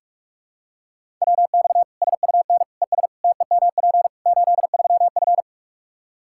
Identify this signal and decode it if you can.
Morse code, heard as a keyed tone, about 41 wpm, 700 Hz. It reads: WX SUN ES TEMP 83F
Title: K1USN SST QSO Practice 45wpm